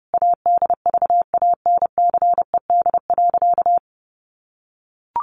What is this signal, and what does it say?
Morse code, heard as a keyed tone, about 30 wpm, 700 Hz. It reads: ADVANCED.
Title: Sentences from Top 1200 Words 30wpm